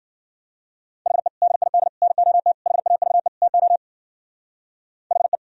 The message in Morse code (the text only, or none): HE DID NOT HAVE TO HE